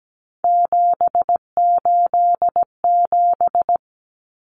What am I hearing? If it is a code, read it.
Morse code, heard as a keyed tone, about 17 wpm, 700 Hz. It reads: 787
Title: Common Cut Numbers - 3-Digit 17wpm